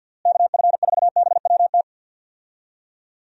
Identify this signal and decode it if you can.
Morse code, heard as a keyed tone, about 51 wpm, 700 Hz. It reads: KF4BWT
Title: USA Call Signs - 6 Characters - Repeated 3x - 50wpm